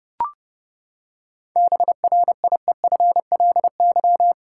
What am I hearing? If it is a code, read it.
Morse code, heard as a keyed tone, about 31 wpm, 700 Hz. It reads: BRIEFLY
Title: Seven Letter Words - 30wpm